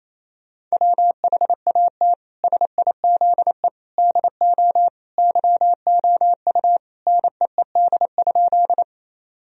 Morse code, heard as a keyed tone, about 28 wpm, 700 Hz. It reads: WHAT SIZE DO YOU NEED?
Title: Sentences from Top 1000 Words - Review Part 2 - 28wpm